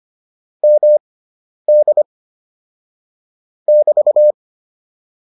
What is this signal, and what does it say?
Morse code, heard as a keyed tone, about 25 wpm, spaced as for 10 wpm, 600 Hz. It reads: MD <BT>